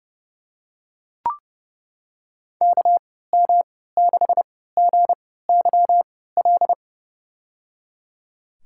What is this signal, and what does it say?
Morse code, heard as a keyed tone, about 30 wpm, spaced as for 17 wpm, 700 Hz. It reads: KM6GYL
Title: USA Call Signs - 3x Character Spacing - 30wpm